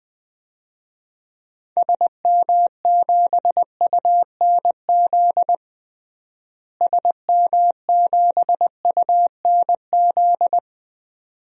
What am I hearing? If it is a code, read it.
Morse code, heard as a keyed tone, about 21 wpm, 700 Hz. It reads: SM7UNZ SM7UNZ
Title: DXCC-100 Call Signs - 6 Characters - Repeated 2x - 20wpm